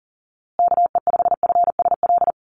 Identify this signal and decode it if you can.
Morse code, heard as a keyed tone, about 40 wpm, 700 Hz. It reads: KE5FSL
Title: USA Call Signs Repeated 3x 40wpm